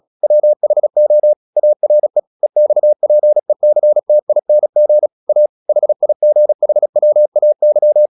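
Morse code, heard as a keyed tone, about 36 wpm, 600 Hz. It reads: WHO ARE EXPECTING A HIGHWAY
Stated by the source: Morse practice at 36 wpm, text of Ken Follett's novel Never, from CW med Gnister.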